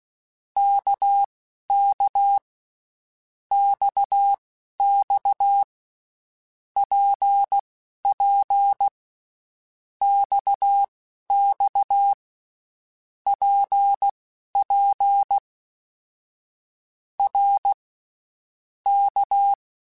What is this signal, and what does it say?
Morse code, heard as a keyed tone, about 16 wpm, spaced as for 12 wpm, 800 Hz. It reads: KK XX PP XX PP R K
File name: Basic Letter Groups By7Groups / RKPX / RKPX@10